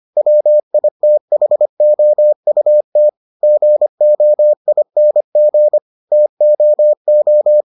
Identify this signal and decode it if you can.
Morse code, heard as a keyed tone, about 25 wpm, 600 Hz. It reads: WITHOUT GOING TOO